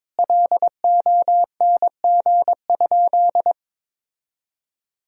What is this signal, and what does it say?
Morse code, heard as a keyed tone, about 22 wpm, 700 Hz. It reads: LONG?
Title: 4x Word-Spacing - Sentences from Top 500 Words - 22wpm